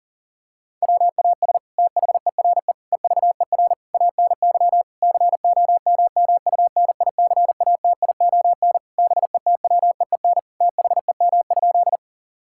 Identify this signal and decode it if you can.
Morse code, heard as a keyed tone, about 40 wpm, 700 Hz. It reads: WAS THERE EVER ANY COMMUNICATION BETWEEN THEM?
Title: Sentences from Top 1000 Words - Review Part 6 - 40wpm